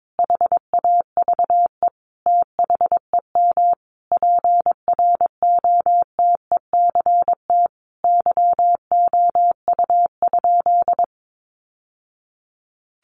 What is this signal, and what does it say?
Morse code, heard as a keyed tone, about 22 wpm, 700 Hz. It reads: HAVE THEM PROTECT YOU?
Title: Sentences from Top 900 Words 22wpm